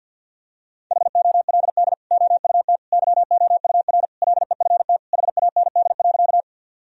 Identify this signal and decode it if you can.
Morse code, heard as a keyed tone, about 50 wpm, 700 Hz. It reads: HOLD OUT YOUR LEFT HAND.
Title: Sentences from Top 1000 Words - Review Part 2 - 50wpm